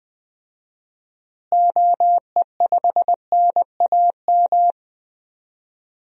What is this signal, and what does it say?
Morse code, heard as a keyed tone, about 20 wpm, 700 Hz. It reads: OE5NAM
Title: DXCC-100 Call Signs - 6 Characters - Repeated 2x - 20wpm